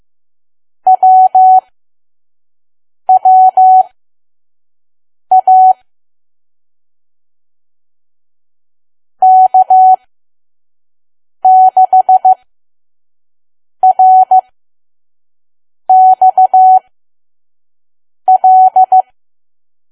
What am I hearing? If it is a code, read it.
Morse code, heard as a keyed tone, about 15 wpm, spaced as for 5 wpm, 750 Hz. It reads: WWA K6RXL
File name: Archive / 5 WPM files / PM